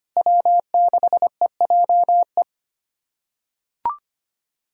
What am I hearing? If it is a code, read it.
Morse code, heard as a keyed tone, about 25 wpm, 700 Hz. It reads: W6EJE
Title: USA Callsign DE Callsign 25wpm